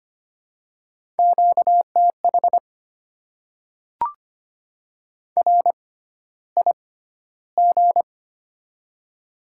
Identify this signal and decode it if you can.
Morse code, heard as a keyed tone, about 25 wpm, spaced as for 13 wpm, 700 Hz. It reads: QTH RIG
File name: Top 20 QSO Elements - 6x Character Spacing - 25wpm